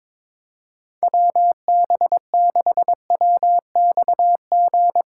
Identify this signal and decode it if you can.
Morse code, heard as a keyed tone, about 21 wpm, 700 Hz. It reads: WB6WXG
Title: USA Callsign DE Callsign 22wpm